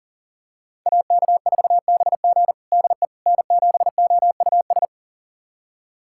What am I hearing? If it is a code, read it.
Morse code, heard as a keyed tone, about 40 wpm, 700 Hz. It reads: AK4BG DE N7OUS
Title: USA Callsign DE Callsign - 5 Characters - 40wpm